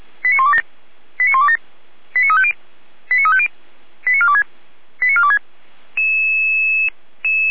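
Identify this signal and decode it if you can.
Five-tone selective call (ZVEI), heard as quick runs of five tones, tones of about 70 ms. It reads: 89127 89127 89350 89350 89426 89426
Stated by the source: ZVEI five-tone call received by SDR, archived by fire brigade Feuerwehr Orlamünde.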